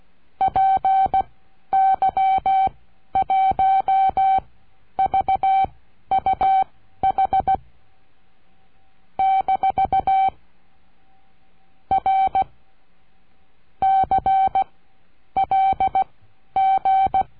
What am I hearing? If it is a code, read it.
Morse code, heard as a keyed tone, about 16 wpm, spaced as for 11 wpm, 770 Hz. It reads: PY1VUH - R CLG
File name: cursos / CURSO DE TELEGRAFIA / Telegrafia - Trilha 22